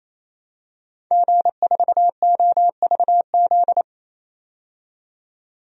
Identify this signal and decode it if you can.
Morse code, heard as a keyed tone, about 28 wpm, 700 Hz. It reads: G4OVZ
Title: DXCC-100 Call Signs - 5 Characters - 28wpm